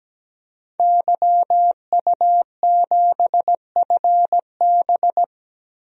Morse code, heard as a keyed tone, about 17 wpm, 700 Hz. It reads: YU7FB
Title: DXCC-100 Call Signs - 5 Characters - 17wpm